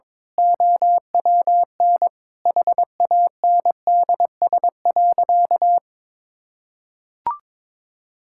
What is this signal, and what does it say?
Morse code, heard as a keyed tone, about 21 wpm, 700 Hz. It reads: OWN HANDS.
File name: practice / 22wpm / Sentences with 8-Word Phrases - 22wpm